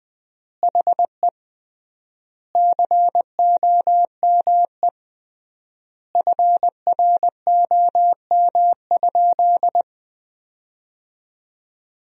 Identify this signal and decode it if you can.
Morse code, heard as a keyed tone, about 20 wpm, 700 Hz. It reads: HE COME FROM?
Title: 3x Word-Spacing - Sentences from Top 300 Words - 20wpm